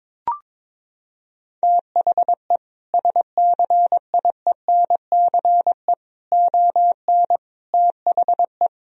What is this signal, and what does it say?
Morse code, heard as a keyed tone, about 22 wpm, 700 Hz. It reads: THE SCIENCE ON THE
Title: Sentences from Top 1000 Words - Review Part 2 - 22wpm